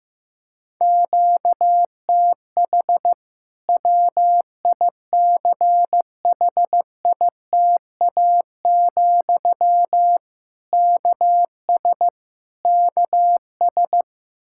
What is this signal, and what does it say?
Morse code, heard as a keyed tone, about 15 wpm, 700 Hz. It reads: QTH WICHITA, KS KS
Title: K1USN SST QSO Practice 15wpm